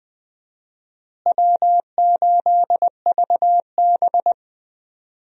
Morse code, heard as a keyed tone, about 20 wpm, 700 Hz. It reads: W8VB
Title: USA Call Signs - 4 Characters - Repeated 3x - 20wpm